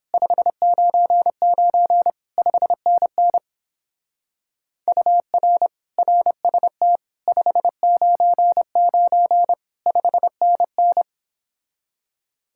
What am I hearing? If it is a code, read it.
Morse code, heard as a keyed tone, about 29 wpm, 700 Hz. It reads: 599 5NN UR RST 599 5NN